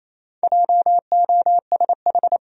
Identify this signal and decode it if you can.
Morse code, heard as a keyed tone, about 28 wpm, 700 Hz. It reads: JOSH
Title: USA Common Names - 28wpm